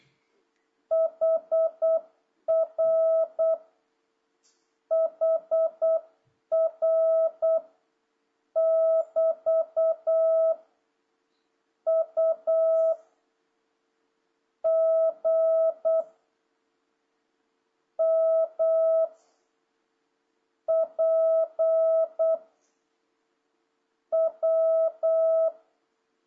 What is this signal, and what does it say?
Morse code, heard as a keyed tone, about 8 wpm, 660 Hz. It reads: HR HR <BT> U G M P W
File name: jh8cbh / 2026cw